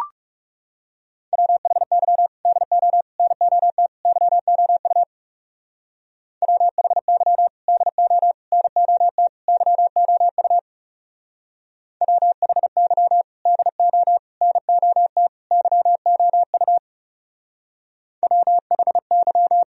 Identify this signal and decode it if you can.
Morse code, heard as a keyed tone, about 36 wpm, 700 Hz. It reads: WHY DO NOT YOU WHY DO NOT YOU WHY DO NOT YOU WHY
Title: Speed-Racing - Sets of 4 Words 30wpm